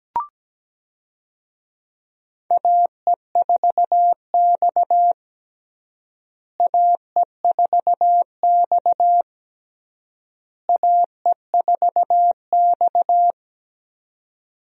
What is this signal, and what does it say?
Morse code, heard as a keyed tone, about 17 wpm, 700 Hz. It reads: AE4X AE4X AE4X
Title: USA Call Signs - 4 Characters - Repeated 3x - 17wpm